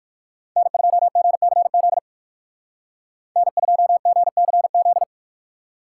Morse code, heard as a keyed tone, about 58 wpm, 700 Hz. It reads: N2GCZ N2GCZ